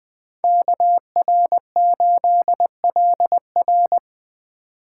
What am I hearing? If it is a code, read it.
Morse code, heard as a keyed tone, about 20 wpm, 700 Hz. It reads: KR8LR
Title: Speed Racing - USA 2x2 Call Signs 17wpm